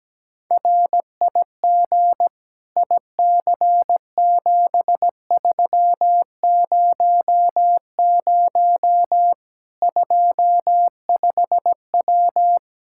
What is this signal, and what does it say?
Morse code, heard as a keyed tone, about 17 wpm, 700 Hz. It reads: RIG IC7300 25W
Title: Practice QSO - Exchange 2 - Info - 17wpm